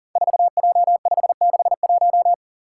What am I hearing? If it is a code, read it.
Morse code, heard as a keyed tone, about 40 wpm, 700 Hz. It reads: 4J561
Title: Five Challenging Letter-Number Combinations 40wpm